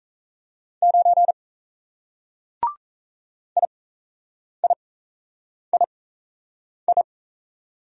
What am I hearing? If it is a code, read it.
Morse code, heard as a keyed tone, about 36 wpm, spaced as for 8 wpm, 700 Hz. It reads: 9 IIII